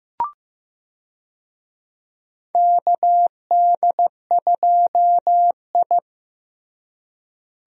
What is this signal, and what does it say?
Morse code, heard as a keyed tone, about 15 wpm, 700 Hz. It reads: KD2I
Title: USA Call Signs - 4 Characters - Repeated 2x - 15wpm